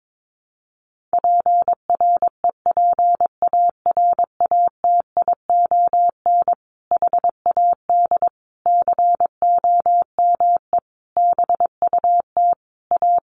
Morse code, heard as a keyed tone, about 22 wpm, 700 Hz. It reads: PREPARATION HAD COME BUT A
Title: Sentences from Top 1400 Words 22wpm